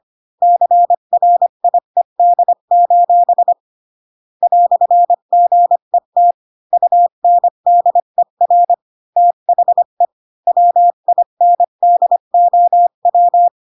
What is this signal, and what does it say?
Morse code, heard as a keyed tone, about 25 wpm, 700 Hz. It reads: CRIED: "GET UNDER THE WINDOW